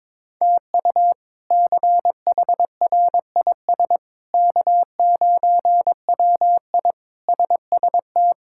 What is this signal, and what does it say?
Morse code, heard as a keyed tone, about 22 wpm, 700 Hz. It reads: TU CHRIS K9WI SST
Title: K1USN SST QSO Practice - 22wpm